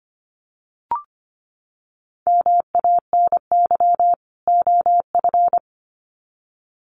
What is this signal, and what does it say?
Morse code, heard as a keyed tone, about 25 wpm, 700 Hz. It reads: MANY OF
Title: Sets of 2 Words 25wpm